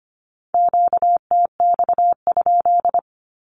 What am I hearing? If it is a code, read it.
Morse code, heard as a keyed tone, about 25 wpm, 700 Hz. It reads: QTX?
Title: Q-Codes with Spoken Meaning - 25wpm